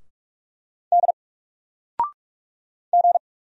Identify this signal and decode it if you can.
Morse code, heard as a keyed tone, about 44 wpm, 700 Hz. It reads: D G
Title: Single Letter-Number 45wpm